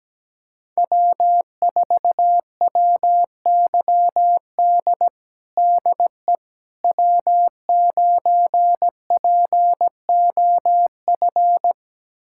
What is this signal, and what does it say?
Morse code, heard as a keyed tone, about 17 wpm, 700 Hz. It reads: W4WYD DE W9POF
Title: USA Callsign DE Callsign - 5 Characters - 17wpm